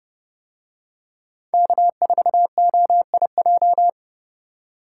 Morse code, heard as a keyed tone, about 29 wpm, 700 Hz. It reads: K4OIJ